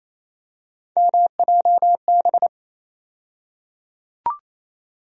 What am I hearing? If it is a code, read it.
Morse code, heard as a keyed tone, about 28 wpm, 700 Hz. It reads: MJB
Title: USA Call Sign Postfixes - 3 Letters 28wpm